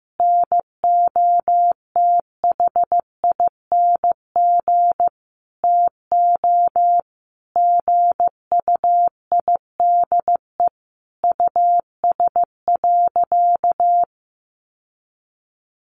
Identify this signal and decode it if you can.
Morse code, heard as a keyed tone, about 15 wpm, 700 Hz. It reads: NOTHING TO GUIDE US.
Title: Sentences from Top 800 Words 15wpm